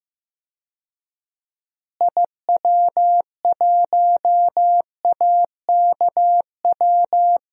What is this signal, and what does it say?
Morse code, heard as a keyed tone, about 15 wpm, 700 Hz. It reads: IW1AKW
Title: DXCC-100 Call Signs - 6 Characters - Repeated 3x - 15wpm